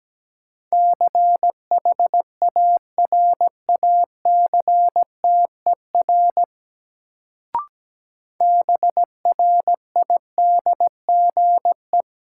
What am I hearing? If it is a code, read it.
Morse code, heard as a keyed tone, about 17 wpm, 700 Hz. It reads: CHARACTER BRIDGE
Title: Sentences from Top 1200 Words - New Word Drill - 17wpm